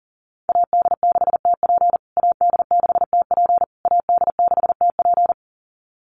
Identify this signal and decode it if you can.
Morse code, heard as a keyed tone, about 40 wpm, 700 Hz. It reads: AD6TP AD6TP AD6TP
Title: USA Call Signs Repeated 3x 40wpm